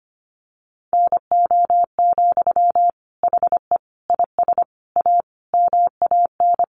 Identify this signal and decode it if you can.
Morse code, heard as a keyed tone, about 25 wpm, 700 Hz. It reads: NO, HE IS A MAN
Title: Sentences from Top 400 Words 25wpm